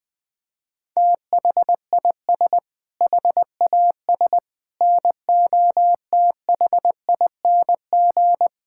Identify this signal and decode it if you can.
Morse code, heard as a keyed tone, about 20 wpm, 700 Hz. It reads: THIS HAS NOTHING
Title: Sentences with 6-Word Phrases - 20wpm